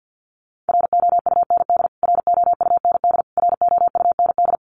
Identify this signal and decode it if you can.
Morse code, heard as a keyed tone, about 50 wpm, 700 Hz. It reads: ROUND ROUND ROUND
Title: Top 500 Words Repeated 50wpm